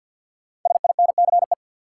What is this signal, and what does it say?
Morse code, heard as a keyed tone, about 53 wpm, 700 Hz. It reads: SINCE
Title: Sentences from Top 300 Words - New Word Drill - 50wpm